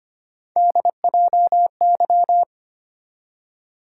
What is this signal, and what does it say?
Morse code, heard as a keyed tone, about 25 wpm, 700 Hz. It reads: DJY
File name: Three Letter-Number Combinations - 25wpm